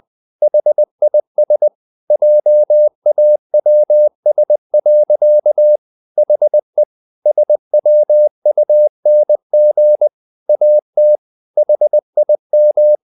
Morse code, heard as a keyed tone, about 20 wpm, 600 Hz. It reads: HIS JAWS. HE SWUNG AT HIM
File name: the old man and the sea part 2 20wpm